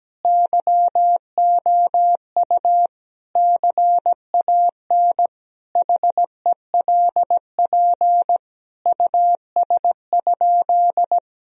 Morse code, heard as a keyed tone, about 17 wpm, 700 Hz. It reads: YOU CAN HELP US?
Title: Sentences from Top 1000 Words - Review Part 3 - 17wpm